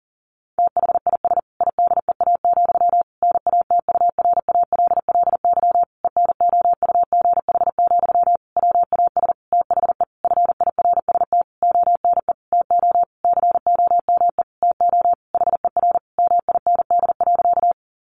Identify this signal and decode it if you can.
Morse code, heard as a keyed tone, about 40 wpm, 700 Hz. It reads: THIS IDEA, NATURALLY ENOUGH, WAS THE FIRST ONE TO COME TO HER MIND.
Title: Sentences from Top 1300 Words 40wpm